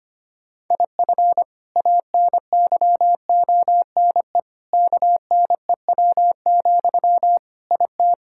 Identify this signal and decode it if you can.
Morse code, heard as a keyed tone, about 25 wpm, 700 Hz. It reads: IF ANYONE KNEW, IT
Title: Sentences from Top 1000 Words - Review Part 6 - 25wpm